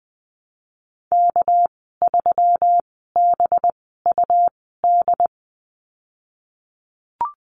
Morse code, heard as a keyed tone, about 20 wpm, spaced as for 15 wpm, 700 Hz. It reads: K3BUD
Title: 219 Lesson 35g - Number 0 - Call Signs 20wpm